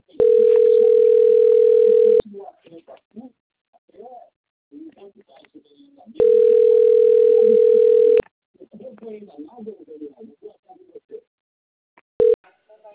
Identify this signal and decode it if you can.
Ringback tone (North America), 440 + 480 Hz, with speech.
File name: out- - - - - .77085